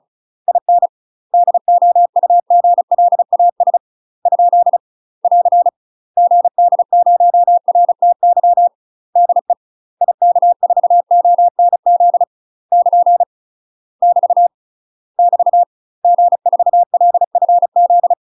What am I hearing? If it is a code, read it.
Morse code, heard as a keyed tone, about 34 wpm, 700 Hz. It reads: IN DOUGLAS ? <AR> GD0RTY DE IK4ONZ <KN> <BT> <BT> G4LFZ
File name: UK MASTER QSOs TRAINING / 35WPM / UK MASTER QSOs 0025